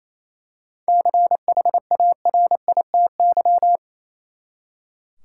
Morse code, heard as a keyed tone, about 28 wpm, 700 Hz. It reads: CHARITY
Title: Seven Letter Words - 28wpm